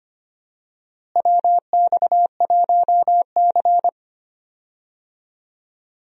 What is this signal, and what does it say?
Morse code, heard as a keyed tone, about 25 wpm, 700 Hz. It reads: WX1C
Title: USA Call Signs - 4 Characters - Repeated 2x - 25wpm